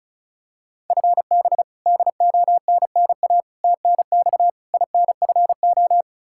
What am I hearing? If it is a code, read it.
Morse code, heard as a keyed tone, about 36 wpm, 700 Hz. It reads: FB DONNA TNX INFO
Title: Practice QSO - Exchange 3 - More Info - 35wpm